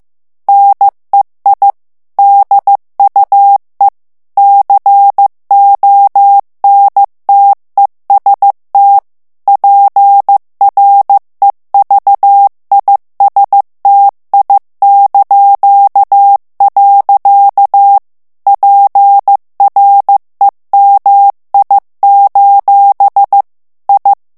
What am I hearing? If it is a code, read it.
Morse code, heard as a keyed tone, about 15 wpm, 800 Hz. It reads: NEI DUE CONTEST PREVISTI). PREMI: I
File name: morse mp3 / ITALIEN texte clair 2 vitesse 14 mots minute